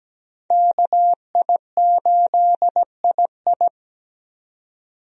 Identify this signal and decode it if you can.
Morse code, heard as a keyed tone, about 17 wpm, 700 Hz. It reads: KI8II
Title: USA Call Signs - 5 Characters - 17wpm